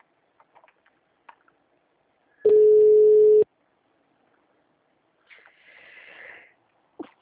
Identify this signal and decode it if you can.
Ringback tone (Europe, 425 Hz).